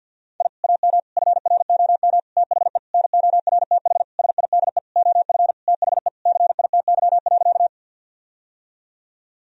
Morse code, heard as a keyed tone, about 52 wpm, 700 Hz. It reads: I AM FROM THE NORTH SIDE OF THE CITY.
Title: Sentences from Top 1000 Words - Review Part 1 - 50wpm